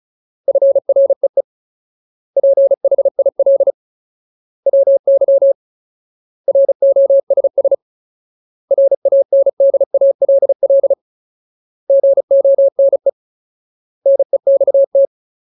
Morse code, heard as a keyed tone, about 34 wpm, 550 Hz. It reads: FREE PHIL WY ROSS RANDALL GONE NEXT
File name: Ultimate Word List / 35WPM Ultimate